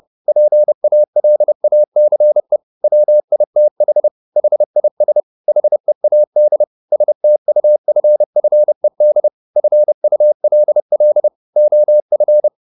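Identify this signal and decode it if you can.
Morse code, heard as a keyed tone, about 30 wpm, 600 Hz. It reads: PALACE WITH HIS HEAD STUFFED FULL OF
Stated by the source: Morse code at 30 wpm, the novel Never by Ken Follett in English, from CW med Gnister.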